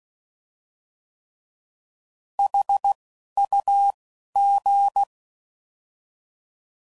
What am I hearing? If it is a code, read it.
Morse code, heard as a keyed tone, about 16 wpm, 800 Hz. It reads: H U G